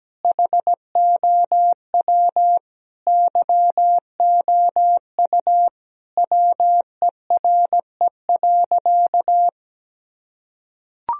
Morse code, heard as a keyed tone, about 17 wpm, 700 Hz. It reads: HOW YOU WERE.
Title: Sentences from Top 1000 Words - Review Part 1 - 17wpm